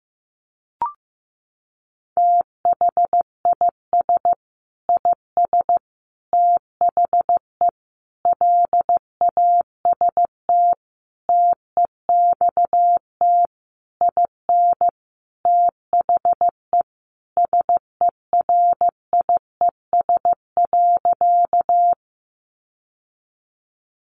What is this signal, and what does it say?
Morse code, heard as a keyed tone, about 15 wpm, 700 Hz. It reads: THIS IS THE LAST TEXT IN THE SERIES.
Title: Sentences from Top 1200 Words 15wpm